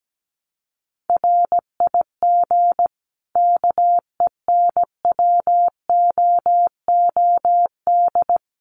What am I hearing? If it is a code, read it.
Morse code, heard as a keyed tone, about 17 wpm, 700 Hz. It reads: RIG KENWOOD
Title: Common QSO Phrases 17wpm